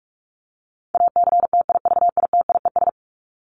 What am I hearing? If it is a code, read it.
Morse code, heard as a keyed tone, about 44 wpm, 700 Hz. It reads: ACTIVITIES
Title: ES:S Suffix - Top 100 Words with Base Words 45wpm